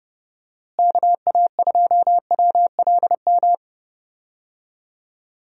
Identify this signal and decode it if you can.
Morse code, heard as a keyed tone, about 32 wpm, 700 Hz. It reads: KA2WLM
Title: USA Call Signs - 6 Characters - Repeated 3x - 30wpm